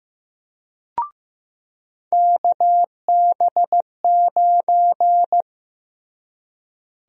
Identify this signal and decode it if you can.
Morse code, heard as a keyed tone, about 15 wpm, 700 Hz. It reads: KB9
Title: USA Call Sign Prefixes - Repeated 2x - 15wpm